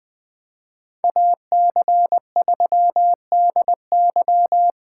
Morse code, heard as a keyed tone, about 24 wpm, 700 Hz. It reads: AC3DY
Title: Speed Racing - USA 2x2 Call Signs 20wpm